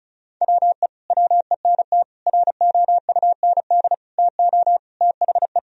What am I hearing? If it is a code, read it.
Morse code, heard as a keyed tone, about 34 wpm, 700 Hz. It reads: WE WENT ROUND TO THE
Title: Sentences from Top 1000 Words - Review Part 1 - 35wpm